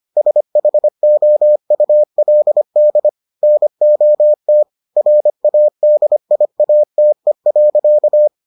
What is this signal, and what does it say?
Morse code, heard as a keyed tone, about 25 wpm, 600 Hz. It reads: SHOULD NOT RADIATE.